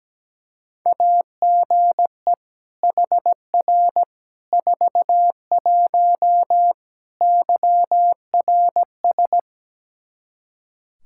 Morse code, heard as a keyed tone, about 17 wpm, 700 Hz. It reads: AGE HR 41 YRS
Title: Practice QSO - Exchange 3 - More Info - 17wpm